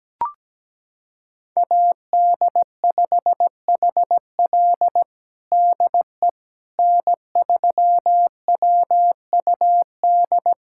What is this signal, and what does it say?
Morse code, heard as a keyed tone, about 17 wpm, 700 Hz. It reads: AD5HL DE N3WUD
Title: USA Callsign DE Callsign - 5 Characters - 17wpm